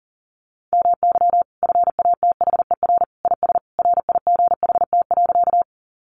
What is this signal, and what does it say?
Morse code, heard as a keyed tone, about 40 wpm, 700 Hz. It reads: MY FATHER IS RIGHT.
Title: Sentences from Top 300 Words 40wpm